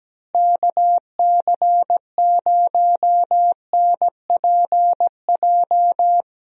Morse code, heard as a keyed tone, about 17 wpm, 700 Hz. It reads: KC0NPJ
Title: USA Call Signs - 6 Characters - Repeated 3x - 17wpm